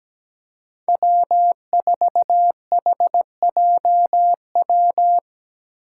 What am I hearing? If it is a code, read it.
Morse code, heard as a keyed tone, about 17 wpm, 700 Hz. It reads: W4HJW
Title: Five Challenging Letter-Number Combinations 17wpm